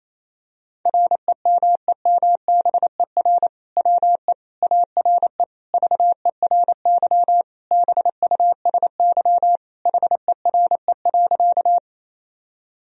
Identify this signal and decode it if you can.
Morse code, heard as a keyed tone, about 28 wpm, 700 Hz. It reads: REMEMBER WE ARE VERY BUSY HERE.
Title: Sentences from Top 1000 Words - Review Part 3 - 28wpm